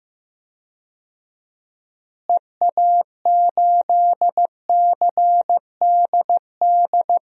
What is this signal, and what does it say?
Morse code, heard as a keyed tone, about 15 wpm, 700 Hz. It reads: EA8CDD
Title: DXCC-100 Call Signs - 6 Characters - Repeated 3x - 15wpm